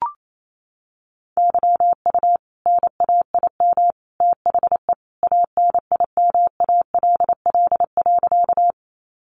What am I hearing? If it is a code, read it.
Morse code, heard as a keyed tone, about 28 wpm, 700 Hz. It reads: YU NAIM THE ANIMALL.
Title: A Sound Of Thunder 28wpm